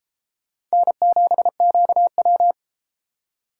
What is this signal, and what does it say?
Morse code, heard as a keyed tone, about 32 wpm, 700 Hz. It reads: N7QW